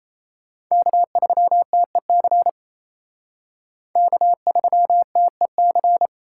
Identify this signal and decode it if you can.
Morse code, heard as a keyed tone, about 35 wpm, 700 Hz. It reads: K3TEC K3TEC